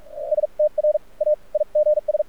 Morse code, heard as a keyed tone, about 46 wpm, spaced as for 33 wpm, 600 Hz. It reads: STRAIGS
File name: archived / CRSnet